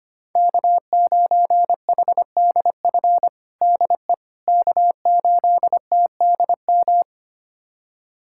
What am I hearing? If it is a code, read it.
Morse code, heard as a keyed tone, about 25 wpm, 700 Hz. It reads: K9HDF DE K8TDM
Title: USA Callsign DE Callsign - 5 Characters - 25wpm